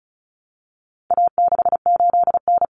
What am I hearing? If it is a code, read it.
Morse code, heard as a keyed tone, about 34 wpm, 700 Hz. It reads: A68N
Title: Common Cut Numbers - 4-Digit 35wpm